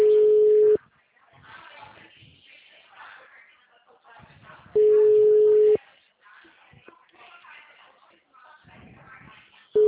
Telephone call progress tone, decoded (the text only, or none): ringback tone